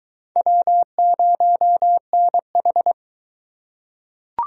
Morse code, heard as a keyed tone, about 20 wpm, 700 Hz. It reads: W0NH